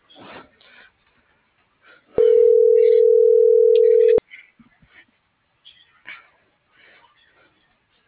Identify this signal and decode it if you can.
Ringback tone (North America), 440 + 480 Hz.